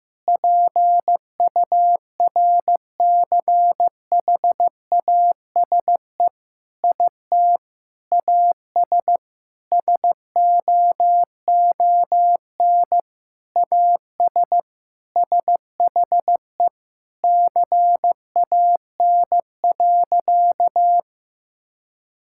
Morse code, heard as a keyed tone, about 15 wpm, 700 Hz. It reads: PURCHASE IT AS SOON AS SHE CAN.
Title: Sentences from Top 1000 Words - Review Part 5 - 15wpm